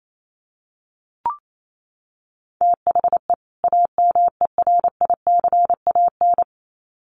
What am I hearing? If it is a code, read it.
Morse code, heard as a keyed tone, about 28 wpm, 700 Hz. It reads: THE AMERICAN
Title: Sets of 2 Words 28wpm